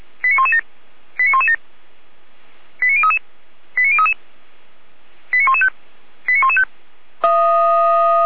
Five-tone selective call (ZVEI), heard as quick runs of five tones, tones of about 80 ms. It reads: 89117 89117 89033 89033 89115 89115